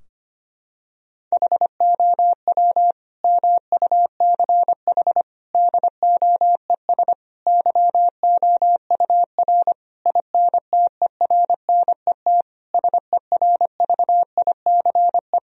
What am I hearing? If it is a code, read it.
Morse code, heard as a keyed tone, about 25 wpm, 700 Hz. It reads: HOW MUCH DOES YOUR INTERNET SERVICE
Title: Sentences from Top 1000 Words - Review Part 4 - 25wpm